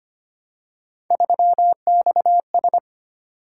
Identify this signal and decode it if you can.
Morse code, heard as a keyed tone, about 25 wpm, 700 Hz. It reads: 3XS
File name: Three Letter-Number Combinations - 25wpm